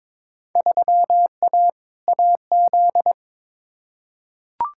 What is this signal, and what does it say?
Morse code, heard as a keyed tone, about 22 wpm, 700 Hz. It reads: 3A AZ
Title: Field Day Exchanges - 22wpm